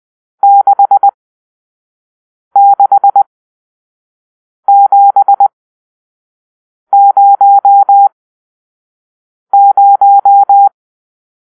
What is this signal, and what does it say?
Morse code, heard as a keyed tone, about 20 wpm, spaced as for 5 wpm, 800 Hz. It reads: 66700
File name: arrl / Grp-7 Drill-3